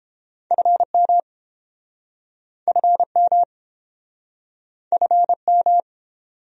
Morse code, heard as a keyed tone, about 30 wpm, 700 Hz. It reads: FM FM FM